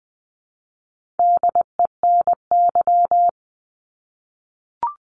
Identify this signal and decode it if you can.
Morse code, heard as a keyed tone, about 20 wpm, 700 Hz. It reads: DENY
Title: 260 Lesson 42e - Words 20-20wpm